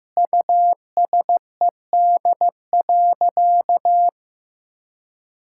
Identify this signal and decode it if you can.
Morse code, heard as a keyed tone, about 15 wpm, 700 Hz. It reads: USED.
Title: Sentences from Top 1600 Words 15wpm